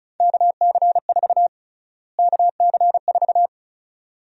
Morse code, heard as a keyed tone, about 34 wpm, 700 Hz. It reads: KC4 KC4
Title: USA Call Sign Prefixes - Repeated 2x - 35wpm